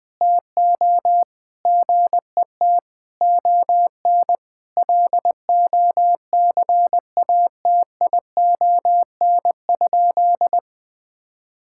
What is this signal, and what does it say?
Morse code, heard as a keyed tone, about 20 wpm, 700 Hz. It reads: TO GET ON LOCATION?
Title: Sentences from Top 1000 Words - Review Part 6 - 20wpm